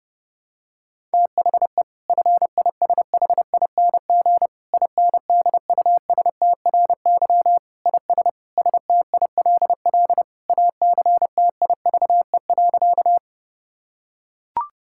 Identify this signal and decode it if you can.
Morse code, heard as a keyed tone, about 30 wpm, 700 Hz. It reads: THE FISHING INDUSTRY IS STILL ACTIVE.
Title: Sentences from Top 1000 Words - Review Part 6 - 30wpm